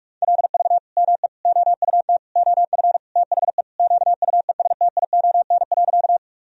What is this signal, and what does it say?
Morse code, heard as a keyed tone, about 46 wpm, 700 Hz. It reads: LV ME OUT OF THE QUESTION.
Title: Sentences from Top 400 Words and Abbreviations 45wpm